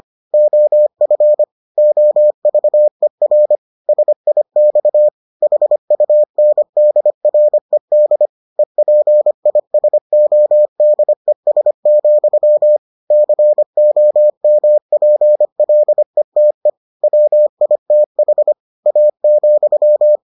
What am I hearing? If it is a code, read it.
Morse code, heard as a keyed tone, about 25 wpm, 600 Hz. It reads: OF OVER SIX HUNDRED EPISODES, COMPLETE WITH A,